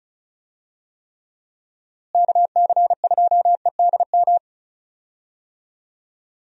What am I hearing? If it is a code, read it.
Morse code, heard as a keyed tone, about 36 wpm, 700 Hz. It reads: KC2EDM
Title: USA Call Signs - 6 Characters - Repeated 2x - 35wpm